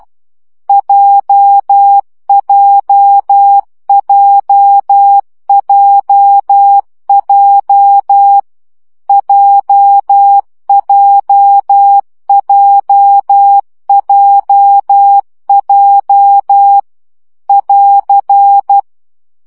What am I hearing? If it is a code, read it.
Morse code, heard as a keyed tone, about 12 wpm, 800 Hz. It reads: JJJJJ JJJJJ <AR>
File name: CW gyakorlas j 10 baud